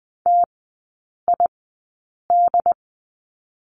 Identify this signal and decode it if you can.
Morse code, heard as a keyed tone, about 19 wpm, spaced as for 8 wpm, 700 Hz. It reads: TID
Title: 100 Lesson 17e - Letter u - Three Characters 20wpm